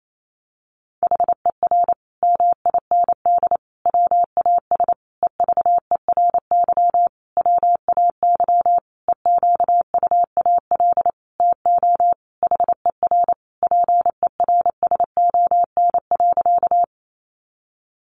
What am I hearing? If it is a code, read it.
Morse code, heard as a keyed tone, about 28 wpm, 700 Hz. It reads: HER MIND WAS EVERY WAY EQUAL TO HER PERSON.
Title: Sentences from Top 1400 Words 28wpm